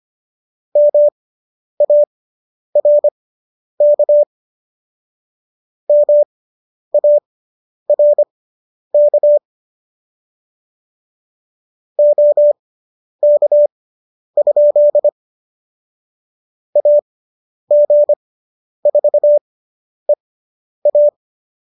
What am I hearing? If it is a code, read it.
Morse code, heard as a keyed tone, about 25 wpm, spaced as for 10 wpm, 590 Hz. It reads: MARK MARK OK? AG4EA